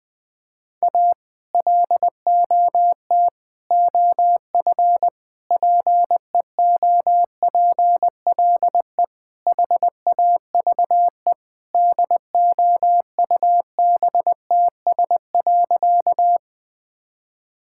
Morse code, heard as a keyed tone, about 21 wpm, 700 Hz. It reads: A LOT OF PEOPLE HAVE DOUBTS.
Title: Sentences with 5-Word Phrases - 20wpm